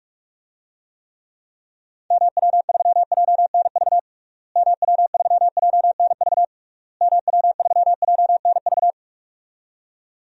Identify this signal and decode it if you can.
Morse code, heard as a keyed tone, about 46 wpm, 700 Hz. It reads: MW3JNV MW3JNV MW3JNV
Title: DXCC-100 Call Signs - 6 Characters - Repeated 3x - 45wpm